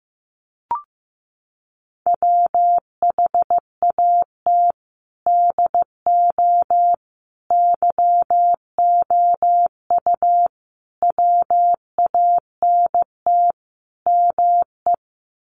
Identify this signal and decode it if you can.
Morse code, heard as a keyed tone, about 15 wpm, 700 Hz. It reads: WHAT DO YOU WANT ME
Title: Sets of 5 Words Encore2 15wpm